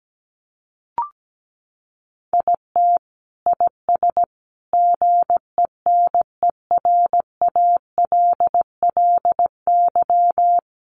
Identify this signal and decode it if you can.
Morse code, heard as a keyed tone, about 17 wpm, 700 Hz. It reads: IT IS GENERALLY
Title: Sentences from Top 1200 Words 17wpm